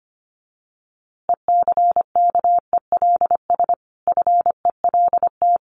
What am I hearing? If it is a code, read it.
Morse code, heard as a keyed tone, about 25 wpm, 700 Hz. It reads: ECKELS FELT
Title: A Sound Of Thunder 25wpm